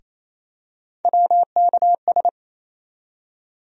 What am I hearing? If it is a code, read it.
Morse code, heard as a keyed tone, about 28 wpm, 700 Hz. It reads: WKS
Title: USA Call Sign Postfixes - 3 Letters 28wpm